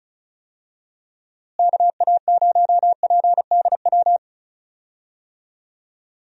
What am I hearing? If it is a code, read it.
Morse code, heard as a keyed tone, about 36 wpm, 700 Hz. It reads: KA0PDW
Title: USA Call Signs - 6 Characters - Repeated 3x - 35wpm